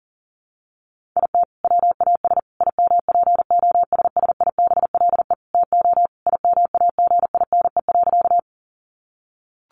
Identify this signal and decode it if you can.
Morse code, heard as a keyed tone, about 40 wpm, 700 Hz. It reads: IT WAS IMPOSSIBLE TO IMAGINE.
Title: Sentences from Top 1400 Words 40wpm